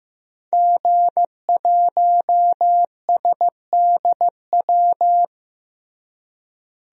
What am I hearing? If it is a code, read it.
Morse code, heard as a keyed tone, about 15 wpm, 700 Hz. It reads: G1SDW
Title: DXCC-100 Call Signs - 5 Characters - Repeated 3x - 15wpm